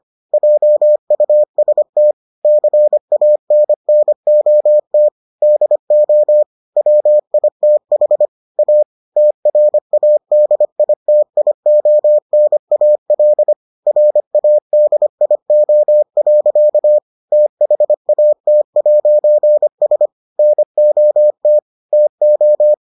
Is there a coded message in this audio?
Yes: Morse code from 0.3 to 22.7 s, heard as a keyed tone, about 25 wpm, 600 Hz. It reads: JUST CANNOT DO WITH A TRADITIONAL RADIO. THAT'S NOT TO